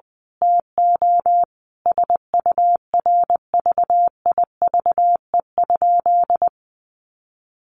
Morse code, heard as a keyed tone, about 20 wpm, 700 Hz. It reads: TO SURVIVE?
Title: A Sound Of Thunder 20wpm